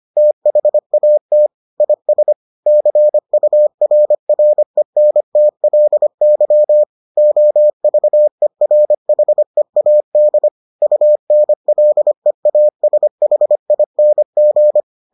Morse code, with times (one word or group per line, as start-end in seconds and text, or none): 0.2-1.5 THAT
1.8-2.3 IS
2.7-6.8 CURRENTLY
7.2-10.5 OVERHEAD
10.8-14.8 UNLEASHING